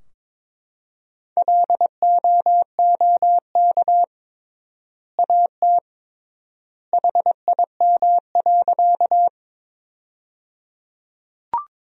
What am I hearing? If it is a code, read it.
Morse code, heard as a keyed tone, about 22 wpm, 700 Hz. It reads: LOOK AT HIM.
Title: 3x Word-Spacing - Sentences from Top 100 Words - 22wpm